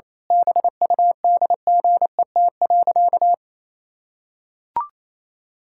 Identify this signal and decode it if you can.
Morse code, heard as a keyed tone, about 27 wpm, 700 Hz. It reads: BUDGET.
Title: Pangram Challenge v2 - 28wpm